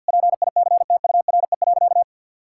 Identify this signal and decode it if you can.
Morse code, heard as a keyed tone, about 50 wpm, 700 Hz. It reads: PICTURE.
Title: Sentences from Top 1600 Words 50wpm